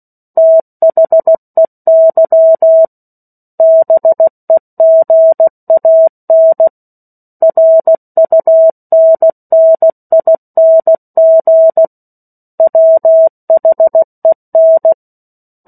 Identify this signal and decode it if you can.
Morse code, heard as a keyed tone, about 16 wpm, 650 Hz. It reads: THEY BEGAN RUNNING WHEN
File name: 03-Chapter-03 16 WPM